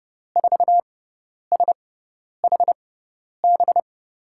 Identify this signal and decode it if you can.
Morse code, heard as a keyed tone, about 30 wpm, spaced as for 10 wpm, 700 Hz. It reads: 4SHB
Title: USA Call Signs - 5 Characters - 6x Character Spacing - 30wpm